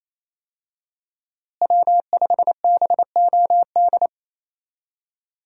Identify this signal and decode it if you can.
Morse code, heard as a keyed tone, about 28 wpm, 700 Hz. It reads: W5BOD
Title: USA Call Signs - 5 Characters - Repeated 3x - 28wpm